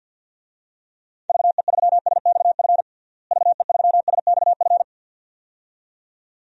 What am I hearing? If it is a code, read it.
Morse code, heard as a keyed tone, about 53 wpm, 700 Hz. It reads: VE3SXF VE3SXF
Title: DXCC-100 Call Signs - 6 Characters - Repeated 2x - 50wpm